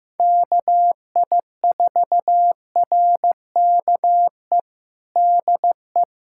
Morse code, heard as a keyed tone, about 15 wpm, 700 Hz. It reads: KI4RKE DE
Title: USA Callsign DE Callsign - 6 Characters - 15wpm